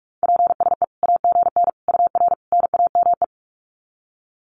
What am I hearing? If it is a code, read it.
Morse code, heard as a keyed tone, about 44 wpm, 700 Hz. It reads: PSE AGN UR NAME
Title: Common QSO Phrases 45wpm